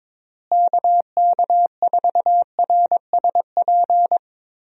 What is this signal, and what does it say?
Morse code, heard as a keyed tone, about 22 wpm, 700 Hz. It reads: KK4RSP
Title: USA Call Signs - 6 Characters - Repeated 2x - 22wpm